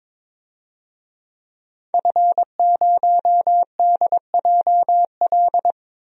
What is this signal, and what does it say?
Morse code, heard as a keyed tone, about 21 wpm, 700 Hz. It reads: F0DJL
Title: DXCC-100 Call Signs - 5 Characters - 22wpm